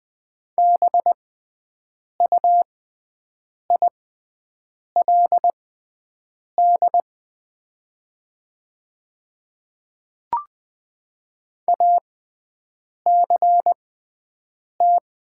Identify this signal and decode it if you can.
Morse code, heard as a keyed tone, about 20 wpm, spaced as for 7 wpm, 700 Hz. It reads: BUILD ACT
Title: 6x Character Spacing - Top 200 Words - 20wpm